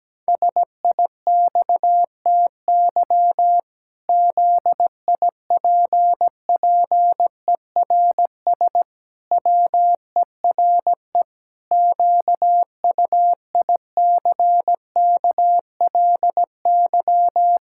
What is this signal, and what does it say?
Morse code, heard as a keyed tone, about 17 wpm, 700 Hz. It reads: SIXTY ZIPPERS WERE QUICKLY
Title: Pangram Challenge v2 - 17wpm